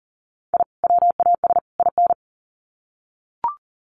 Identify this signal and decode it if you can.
Morse code, heard as a keyed tone, about 40 wpm, 700 Hz. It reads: I WAS IN
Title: Sets of 3 Words 40wpm